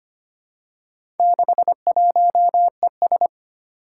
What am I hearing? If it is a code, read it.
Morse code, heard as a keyed tone, about 25 wpm, 700 Hz. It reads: 61ES
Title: Four Challenging Letter-Number Combinations 25wpm